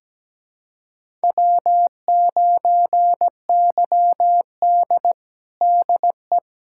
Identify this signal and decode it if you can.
Morse code, heard as a keyed tone, about 17 wpm, 700 Hz. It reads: W9YD DE
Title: USA Callsign DE Callsign - 4 Characters - 17wpm